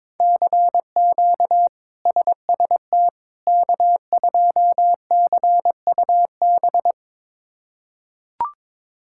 Morse code, heard as a keyed tone, about 22 wpm, 700 Hz. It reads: CQ SST K2CUB
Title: K1USN SST QSO Practice - 22wpm